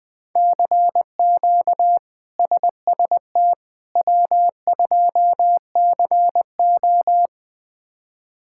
Morse code, heard as a keyed tone, about 20 wpm, 700 Hz. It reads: CQ SST W2CO
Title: K1USN SST QSO Practice - 20wpm